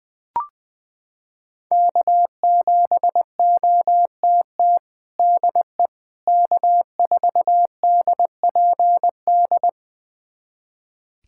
Morse code, heard as a keyed tone, about 21 wpm, 700 Hz. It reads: K7OTT DE K4DPD
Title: USA Callsign DE Callsign - 5 Characters - 20wpm